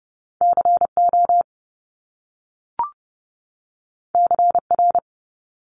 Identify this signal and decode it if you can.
Morse code, heard as a keyed tone, about 30 wpm, 700 Hz. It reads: C O C R
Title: Top Two Letter Combinations in Words 30wpm